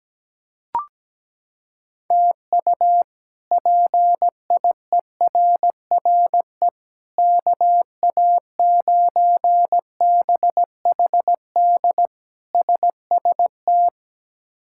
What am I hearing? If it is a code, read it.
Morse code, heard as a keyed tone, about 17 wpm, 700 Hz. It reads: TU PIERRE KA9BHD SST
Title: K1USN SST QSO Practice - 17wpm